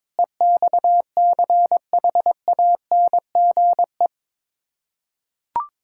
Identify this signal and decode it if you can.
Morse code, heard as a keyed tone, about 22 wpm, 700 Hz. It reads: EXCHANGE
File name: practice / 22wpm / Build A Phrase - 5-Word Phrases - 22wpm